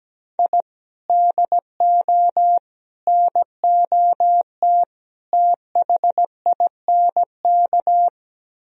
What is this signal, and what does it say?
Morse code, heard as a keyed tone, about 17 wpm, 700 Hz. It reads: I DO NOT THINK
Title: Speed-Racing - Sets of 4 Words 17wpm